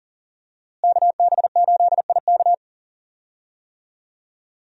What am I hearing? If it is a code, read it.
Morse code, heard as a keyed tone, about 40 wpm, 700 Hz. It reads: KB8IK
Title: USA Call Signs - 5 Characters - Repeated 3x - 40wpm